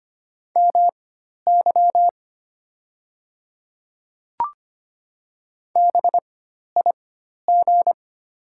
Morse code, heard as a keyed tone, about 25 wpm, spaced as for 12 wpm, 700 Hz. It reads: MY BIG
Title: 4x Character Spacing - Top 300 Words - 25wpm